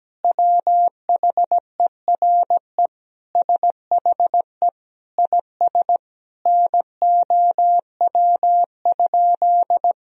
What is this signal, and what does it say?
Morse code, heard as a keyed tone, about 17 wpm, 700 Hz. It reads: WHERE SHE IS NOW?
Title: Sentences from Top 1000 Words - Review Part 2 - 17wpm